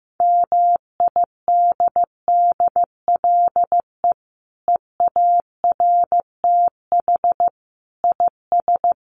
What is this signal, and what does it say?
Morse code, heard as a keyed tone, about 15 wpm, 700 Hz. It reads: MIDDLE EARTH IS
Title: Sentences from Top 900 Words 15wpm